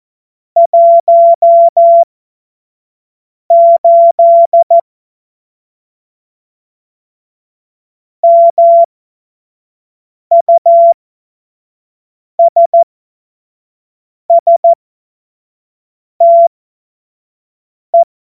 Morse code, heard as a keyed tone, about 14 wpm, spaced as for 5 wpm, 680 Hz. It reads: 18 MUSSTE
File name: Morsekurs / Witze / Witz 19 März 2024 25 F70